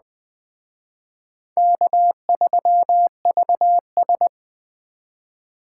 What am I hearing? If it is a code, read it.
Morse code, heard as a keyed tone, about 21 wpm, 700 Hz. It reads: K3VS
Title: Speed Racing - USA 1x2 Call Signs 17wpm